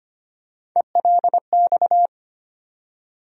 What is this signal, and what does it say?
Morse code, heard as a keyed tone, about 25 wpm, 700 Hz. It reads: ELX
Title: USA Call Sign Postfixes - 3 Letters 25wpm